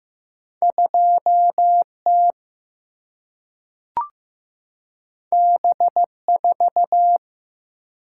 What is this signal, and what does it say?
Morse code, heard as a keyed tone, about 15 wpm, 700 Hz. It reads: 2 T B 4
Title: Single-Letter Single-Number 15wpm